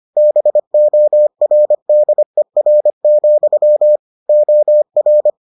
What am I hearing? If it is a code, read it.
Morse code, heard as a keyed tone, about 25 wpm, 600 Hz. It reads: BORDER, OR